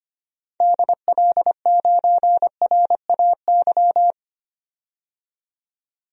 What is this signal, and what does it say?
Morse code, heard as a keyed tone, about 25 wpm, 700 Hz. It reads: DL9RAY
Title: DXCC-100 Call Signs - 6 Characters - 25wpm